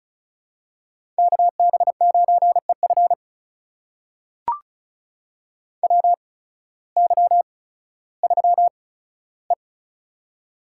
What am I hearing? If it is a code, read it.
Morse code, heard as a keyed tone, about 36 wpm, spaced as for 16 wpm, 700 Hz. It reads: KB9EF WY3E
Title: USA Call Signs - 8x Character Spacing - 35wpm